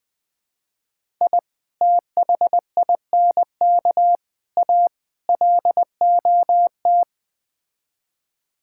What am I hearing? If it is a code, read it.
Morse code, heard as a keyed tone, about 21 wpm, 700 Hz. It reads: I THINK A LOT
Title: Build A Phrase - 5-Word Phrases - 20wpm